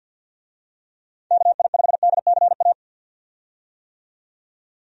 Morse code, heard as a keyed tone, about 52 wpm, 700 Hz. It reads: KI5DCA
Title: USA Call Signs - 6 Characters - 50wpm